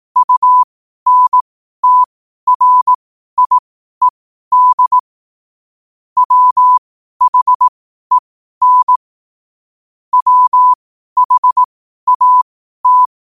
Morse code, heard as a keyed tone, about 18 wpm, spaced as for 12 wpm, 1000 Hz. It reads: UNTRIED WHEN WHAT
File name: qotdfiles / 12wpm